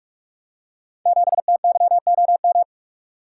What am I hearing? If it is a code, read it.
Morse code, heard as a keyed tone, about 45 wpm, 700 Hz. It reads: 7TYOM
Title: Code Groups - 5 Letter:Numbers - 45wpm